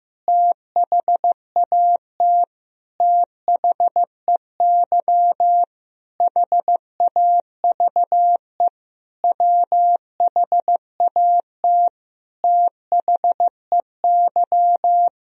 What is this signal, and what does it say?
Morse code, heard as a keyed tone, about 15 wpm, 700 Hz. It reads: THAT THEY HAVE WHAT THEY
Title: Sentences from Top 1000 Words - Review Part 2 - 15wpm